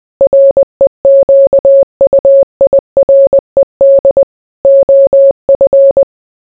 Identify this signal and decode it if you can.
Morse code, heard as a keyed tone, about 20 wpm, 550 Hz. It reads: REQUIRED OF